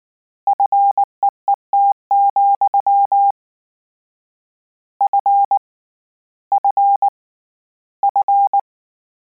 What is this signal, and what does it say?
Morse code, heard as a keyed tone, about 19 wpm, 800 Hz. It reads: FEET, F F F
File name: Combinations / Combinations-mp3s / Combinations@18